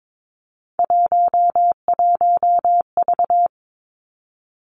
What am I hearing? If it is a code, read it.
Morse code, heard as a keyed tone, about 22 wpm, 700 Hz. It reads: 11V
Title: Three Challenging Letter-Number Combinations 22wpm